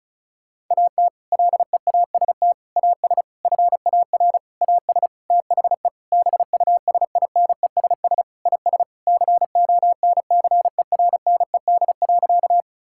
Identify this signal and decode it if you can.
Morse code, heard as a keyed tone, about 36 wpm, 700 Hz. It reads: AT LEAST AS FAR AS THE BUSINESS IS CONCERNED.
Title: Sentences from Top 1000 Words - Review Part 6 - 35wpm